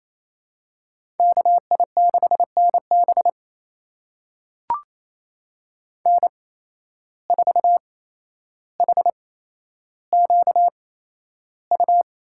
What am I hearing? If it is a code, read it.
Morse code, heard as a keyed tone, about 28 wpm, spaced as for 12 wpm, 700 Hz. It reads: KI6NB N4HQU
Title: USA Call Signs - 5 Characters - 8x Character Spacing - 28wpm